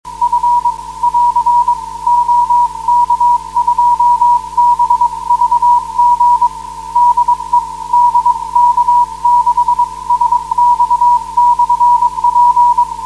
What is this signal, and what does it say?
Morse code, heard as a keyed tone, about 22 wpm, 960 Hz. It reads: F <AR> OK2BVG DE DK6SX/P
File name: 472khz web swl / 23november2012 / dk6sx 472khz